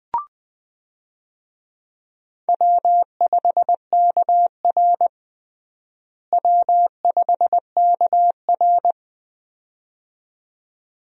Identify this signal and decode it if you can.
Morse code, heard as a keyed tone, about 21 wpm, 700 Hz. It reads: W5KR W5KR
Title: USA Call Signs - 4 Characters - Repeated 2x - 20wpm